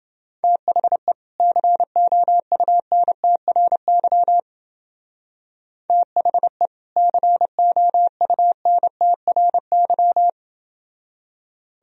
Morse code, heard as a keyed tone, about 28 wpm, 700 Hz. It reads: THE COUNTRY THE COUNTRY